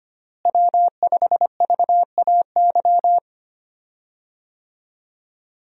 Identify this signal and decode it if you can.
Morse code, heard as a keyed tone, about 25 wpm, 700 Hz. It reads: W5VAY
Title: USA Call Signs - 5 Characters - Repeated 2x - 25wpm